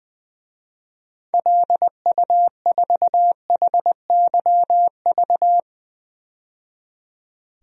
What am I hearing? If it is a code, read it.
Morse code, heard as a keyed tone, about 20 wpm, 700 Hz. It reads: LU4HYV
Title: DXCC-100 Call Signs - 6 Characters - 20wpm